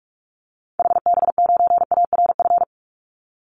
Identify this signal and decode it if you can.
Morse code, heard as a keyed tone, about 44 wpm, 700 Hz. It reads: HB9ARF
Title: Call Signs Repeated 3x - 45wpm